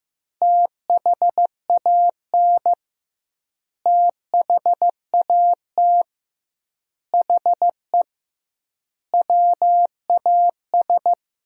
Morse code, heard as a keyed tone, about 15 wpm, 700 Hz. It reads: THAN THAT HE WAS
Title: 2x Word-Spacing - Sentences from Top 200 Words - 15wpm